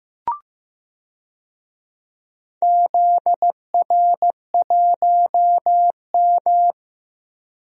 Morse code, heard as a keyed tone, about 15 wpm, 700 Hz. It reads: ZR1M
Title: DXCC-100 Call Signs - 4 Characters - Repeated 3x - 15wpm